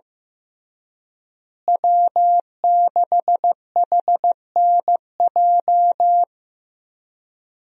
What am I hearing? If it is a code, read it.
Morse code, heard as a keyed tone, about 15 wpm, 700 Hz. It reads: W6HNJ
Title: USA Call Signs - 5 Characters - Repeated 3x - 15wpm